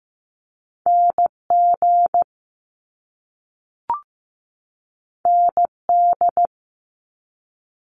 Morse code, heard as a keyed tone, about 15 wpm, 700 Hz. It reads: NG ND
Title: Top Two Letter Combinations in Words 15wpm